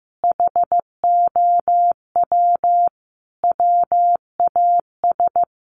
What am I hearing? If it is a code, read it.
Morse code, heard as a keyed tone, about 15 wpm, 700 Hz. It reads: HOW WAS
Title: Sentences from Top 1100 Words 15wpm